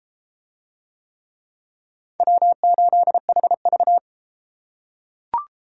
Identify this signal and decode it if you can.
Morse code, heard as a keyed tone, about 33 wpm, 700 Hz. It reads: W8HV